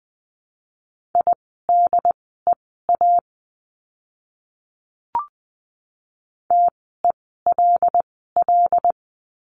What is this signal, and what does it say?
Morse code, heard as a keyed tone, about 20 wpm, spaced as for 15 wpm, 700 Hz. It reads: IDEA TELL
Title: 108 Lesson 18f - Letter m - Words 20wpm